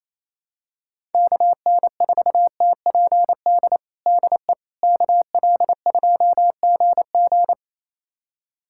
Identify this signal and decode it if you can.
Morse code, heard as a keyed tone, about 28 wpm, 700 Hz. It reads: KN4TPD DE KL2GG
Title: USA Callsign DE Callsign 28wpm